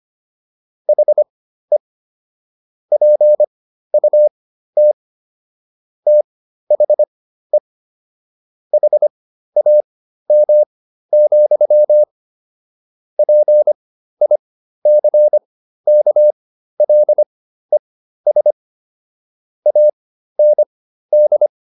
Morse code, heard as a keyed tone, about 25 wpm, spaced as for 13 wpm, 600 Hz. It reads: HE PUT THE HAM, PICKLES AND